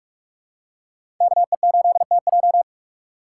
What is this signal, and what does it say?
Morse code, heard as a keyed tone, about 46 wpm, 700 Hz. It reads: KE8TJ
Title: USA Call Signs - 5 Characters - 2x Character Spacing - 45wpm